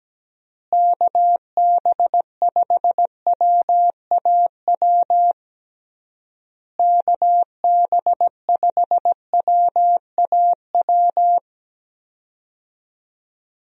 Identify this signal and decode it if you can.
Morse code, heard as a keyed tone, about 17 wpm, 700 Hz. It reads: KB5WAW KB5WAW
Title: USA Call Signs - 6 Characters - Repeated 3x - 17wpm